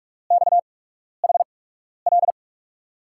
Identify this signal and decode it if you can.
Morse code, heard as a keyed tone, about 44 wpm, spaced as for 12 wpm, 700 Hz. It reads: XHL
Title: USA Call Signs - 8x Character Spacing - 45wpm